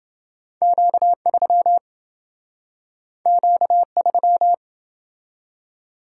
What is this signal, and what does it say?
Morse code, heard as a keyed tone, about 28 wpm, 700 Hz. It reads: Q3 Q3